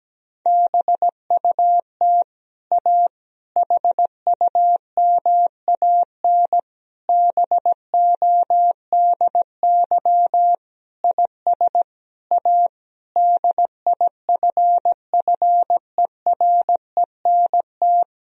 Morse code, heard as a keyed tone, about 17 wpm, 700 Hz. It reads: BUT A HUMAN BODY IS A DIFFERENT
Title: Sentences from Top 1000 Words - Review Part 3 - 17wpm